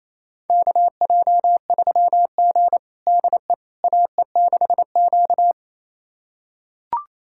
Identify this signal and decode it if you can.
Morse code, heard as a keyed tone, about 28 wpm, 700 Hz. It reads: KJ3G DE AE6Q
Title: USA Callsign DE Callsign - 4 Characters - 28wpm